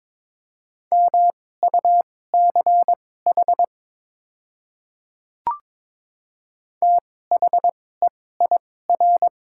Morse code, heard as a keyed tone, about 22 wpm, spaced as for 16 wpm, 700 Hz. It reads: MUCH THEIR
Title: 2x Character Spacing - Top 200 Words - 22wpm